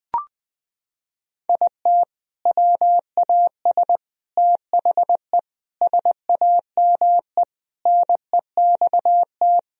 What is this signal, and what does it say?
Morse code, heard as a keyed tone, about 20 wpm, 700 Hz. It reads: IT WAS THE SAME NEXT
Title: Sentences from Top 500 Words and Abbreviations 20wpm